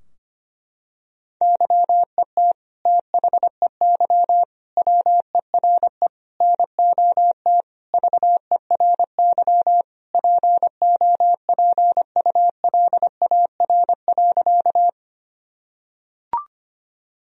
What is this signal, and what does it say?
Morse code, heard as a keyed tone, about 25 wpm, 700 Hz. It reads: YET THEY WERE NOT VERY POPULAR.
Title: Sentences from Top 1000 Words - Review Part 3 - 25wpm